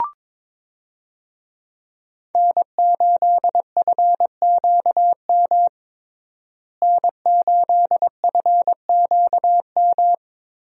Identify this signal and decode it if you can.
Morse code, heard as a keyed tone, about 22 wpm, 700 Hz. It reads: N8FQM N8FQM
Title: USA Call Signs - 5 Characters - Repeated 2x - 22wpm